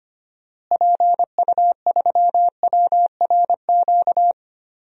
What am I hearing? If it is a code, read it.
Morse code, heard as a keyed tone, about 25 wpm, 700 Hz. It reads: PU3WRQ
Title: DXCC-100 Call Signs - 6 Characters - Repeated 2x - 25wpm